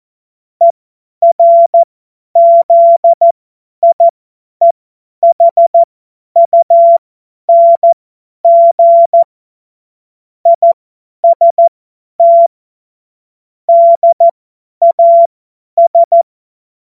Morse code, heard as a keyed tone, about 14 wpm, spaced as for 10 wpm, 680 Hz. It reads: ERZIEHUNG IST DAS
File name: Morsekurs / Witze / Witz 15. November 2022 50 F70